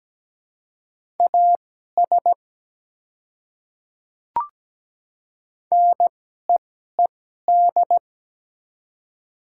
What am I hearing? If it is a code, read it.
Morse code, heard as a keyed tone, about 17 wpm, spaced as for 12 wpm, 700 Hz. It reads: AS NEED
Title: 2x Character Spacing - Top 200 Words - 17wpm